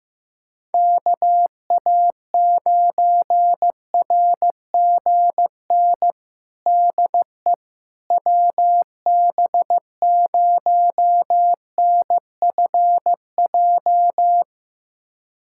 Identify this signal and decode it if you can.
Morse code, heard as a keyed tone, about 16 wpm, 700 Hz. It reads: KA9RGN DE WB0NFJ
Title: USA Callsign DE Callsign - 6 Characters - 15wpm